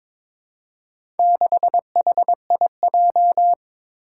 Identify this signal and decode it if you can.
Morse code, heard as a keyed tone, about 22 wpm, 700 Hz. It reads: 6HIJ
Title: Four Challenging Letter-Number Combinations 22wpm